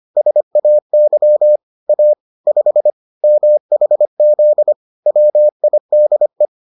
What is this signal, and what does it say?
Morse code, heard as a keyed tone, about 25 wpm, 600 Hz. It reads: SAY A 5 MHZ WIDE